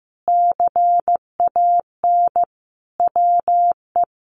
Morse code, heard as a keyed tone, about 15 wpm, 700 Hz. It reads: CAN WE
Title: Sentences from Top 1300 Words 15wpm